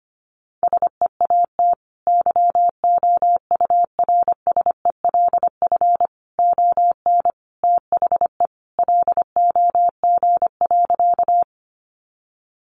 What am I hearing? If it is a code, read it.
Morse code, heard as a keyed tone, about 25 wpm, 700 Hz. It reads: SEAT YOURSELF ON THE LOG.
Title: Sentences from Top 1500 Words 25wpm